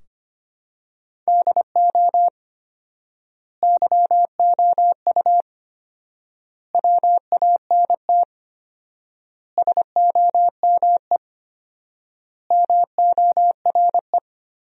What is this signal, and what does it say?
Morse code, heard as a keyed tone, about 25 wpm, 700 Hz. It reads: DO YOU WANT SOME MORE
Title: 4x Word-Spacing - Sentences from Top 300 Words - 25wpm